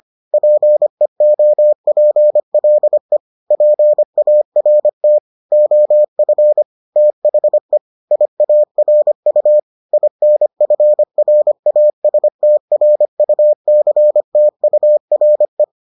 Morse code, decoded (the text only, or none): PEOPLE PART OF THE IARU INFRASTRUCTURE